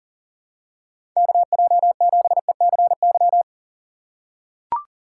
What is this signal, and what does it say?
Morse code, heard as a keyed tone, about 40 wpm, 700 Hz. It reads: KJ7ECY
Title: USA Call Signs - 6 Characters - 6x Character Spacing - 40wpm